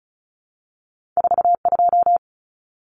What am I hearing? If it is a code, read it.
Morse code, heard as a keyed tone, about 36 wpm, 700 Hz. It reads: 42
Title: Numbers - 2-Digit 35wpm